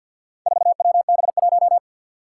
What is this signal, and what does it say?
Morse code, heard as a keyed tone, about 51 wpm, 700 Hz. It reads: 4WB1
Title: Four Challenging Letter-Number Combinations 50wpm